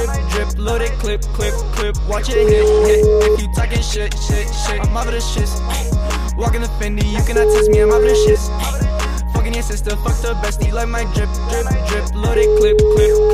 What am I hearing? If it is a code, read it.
Ringback tone (Europe, 425 Hz), with noise.